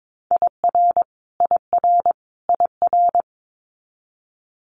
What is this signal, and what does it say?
Morse code, heard as a keyed tone, about 22 wpm, 700 Hz. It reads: IR IR IR
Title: Multi-letter Phonemes Repeated 3x 22wpm